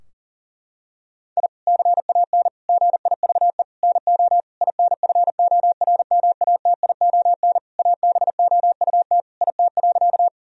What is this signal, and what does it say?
Morse code, heard as a keyed tone, about 37 wpm, 700 Hz. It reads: I CAN GIVE NO INFORMATION ABOUT IT.
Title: Sentences from Top 1000 Words - Review Part 3 - 40wpm